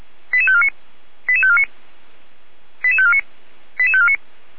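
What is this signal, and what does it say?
Five-tone selective call (ZVEI), heard as quick runs of five tones, tones of about 70 ms. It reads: 88549 88549 88549 88549
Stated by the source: ZVEI five-tone selective call received on an SDR.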